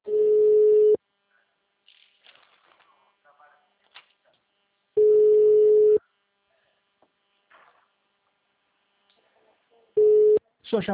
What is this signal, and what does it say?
Ringback tone (Europe, 425 Hz), with speech.